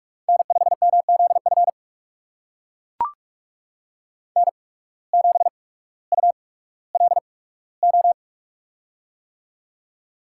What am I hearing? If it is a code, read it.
Morse code, heard as a keyed tone, about 44 wpm, spaced as for 19 wpm, 700 Hz. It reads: N5MZF N7ULO
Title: USA Call Signs - 5 Characters - 8x Character Spacing - 45wpm